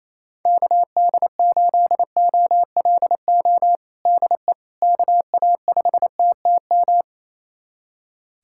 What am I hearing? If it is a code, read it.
Morse code, heard as a keyed tone, about 28 wpm, 700 Hz. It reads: KD8OLO DE KA5TTM
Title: USA Callsign DE Callsign 28wpm